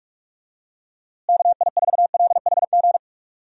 Morse code, heard as a keyed tone, about 46 wpm, 700 Hz. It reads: KI4LHG
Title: USA Call Signs - 6 Characters - Repeated 2x - 45wpm